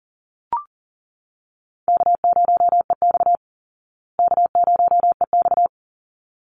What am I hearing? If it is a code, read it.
Morse code, heard as a keyed tone, about 40 wpm, 700 Hz. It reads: K0EX K0EX
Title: Easy Call Signs Repeated 2x - 40wpm